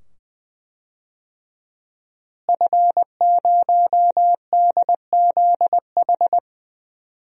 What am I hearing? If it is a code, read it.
Morse code, heard as a keyed tone, about 20 wpm, 700 Hz. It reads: F0DZH
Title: DXCC-100 Call Signs - 5 Characters - Repeated 2x - 20wpm